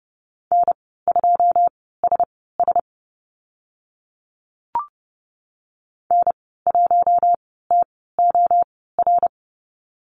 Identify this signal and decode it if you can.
Morse code, heard as a keyed tone, about 30 wpm, spaced as for 17 wpm, 700 Hz. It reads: N2SS N1TOR
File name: Lesson 14g - Number 2 - Call Signs 30wpm